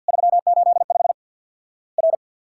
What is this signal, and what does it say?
Morse code, heard as a keyed tone, about 50 wpm, 700 Hz. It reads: 385 R
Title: Worked All Europe DX Contest - QTC Exchange - 50wpm